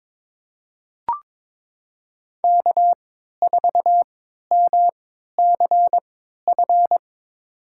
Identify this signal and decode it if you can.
Morse code, heard as a keyed tone, about 22 wpm, spaced as for 12 wpm, 700 Hz. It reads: K4MCF
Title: USA Call Signs - 5 Characters - 3x Character Spacing - 22wpm